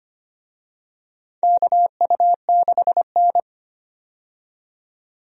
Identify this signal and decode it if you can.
Morse code, heard as a keyed tone, about 25 wpm, 700 Hz. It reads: KU6N
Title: USA Call Signs - 4 Characters - 25wpm